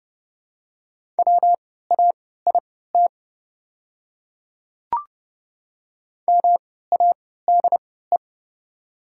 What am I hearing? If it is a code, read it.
Morse code, heard as a keyed tone, about 30 wpm, spaced as for 17 wpm, 700 Hz. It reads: WAIT MADE
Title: 3x Character Spacing - Top 500 Words - 30wpm